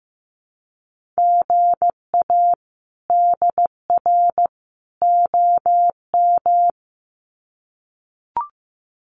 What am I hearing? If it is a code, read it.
Morse code, heard as a keyed tone, about 15 wpm, 700 Hz. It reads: GA DR OM
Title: Common QSO Phrases 15wpm